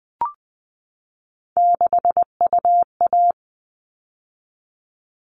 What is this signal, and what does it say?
Morse code, heard as a keyed tone, about 20 wpm, 700 Hz. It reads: 6UA
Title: All Cut Numbers - 3-Digit 20wpm